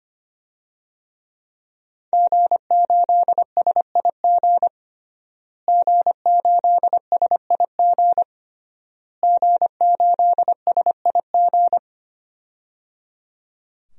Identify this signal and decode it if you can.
Morse code, heard as a keyed tone, about 25 wpm, 700 Hz. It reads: G8SIG G8SIG G8SIG
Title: DXCC-100 Call Signs - 5 Characters - Repeated 3x - 25wpm